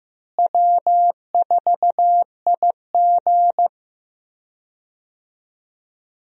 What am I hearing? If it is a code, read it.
Morse code, heard as a keyed tone, about 15 wpm, 700 Hz. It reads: W4IG
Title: USA Call Signs - 4 Characters - Repeated 3x - 15wpm